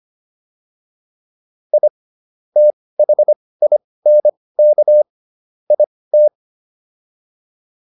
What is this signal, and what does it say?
Morse code, heard as a keyed tone, about 25 wpm, spaced as for 18 wpm, 600 Hz. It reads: I THINK IT